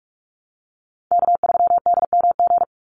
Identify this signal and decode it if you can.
Morse code, heard as a keyed tone, about 44 wpm, 700 Hz. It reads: K3DMG
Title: Call Signs - 45wpm